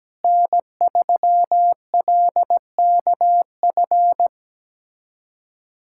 Morse code, heard as a keyed tone, about 17 wpm, 700 Hz. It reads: N3LKF
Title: USA Callsign DE Callsign - 5 Characters - 17wpm